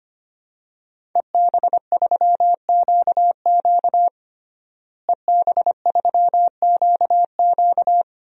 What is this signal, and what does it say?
Morse code, heard as a keyed tone, about 25 wpm, 700 Hz. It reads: EB3QQ EB3QQ
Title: DXCC-100 Call Signs - 5 Characters - Repeated 2x - 25wpm